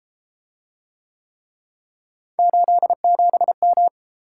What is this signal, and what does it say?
Morse code, heard as a keyed tone, about 31 wpm, 700 Hz. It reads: 87M